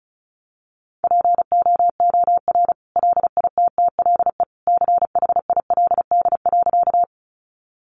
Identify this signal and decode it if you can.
Morse code, heard as a keyed tone, about 34 wpm, 700 Hz. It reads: POOR LITTLE CHILD.
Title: Sentences from Top 1000 Words 35wpm